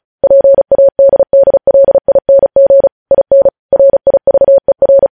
Morse code, heard as a keyed tone, about 36 wpm, 550 Hz. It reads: PADDLING IN RIVER
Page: news headlines in morse code at 35 wpm